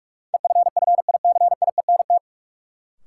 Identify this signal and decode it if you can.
Morse code, heard as a keyed tone, about 43 wpm, 700 Hz. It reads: EFFICIENT
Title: Sentences from Top 1300 Words - New Word Drill Alt - 45wpm